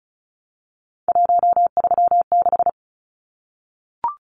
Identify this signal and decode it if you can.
Morse code, heard as a keyed tone, about 34 wpm, 700 Hz. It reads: 136
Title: Numbers - 3-Digit 35wpm